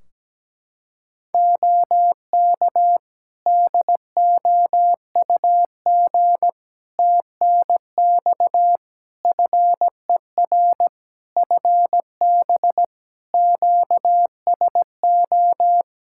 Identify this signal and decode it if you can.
Morse code, heard as a keyed tone, about 17 wpm, 700 Hz. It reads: OK DOUG TNX FER FB QSO
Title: K1USN SST QSO Practice 17wpm